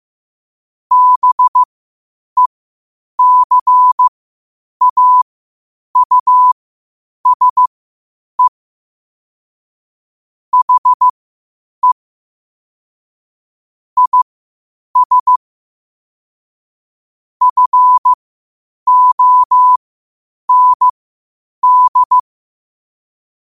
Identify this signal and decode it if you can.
Morse code, heard as a keyed tone, about 15 wpm, spaced as for 8 wpm, 1000 Hz. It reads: BECAUSE HE IS FOND
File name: qotdfiles / 8wpm